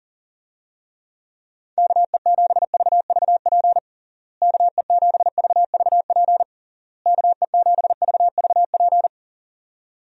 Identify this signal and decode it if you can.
Morse code, heard as a keyed tone, about 40 wpm, 700 Hz. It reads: KE7VVP KE7VVP KE7VVP
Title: USA Call Signs - 6 Characters - Repeated 3x - 40wpm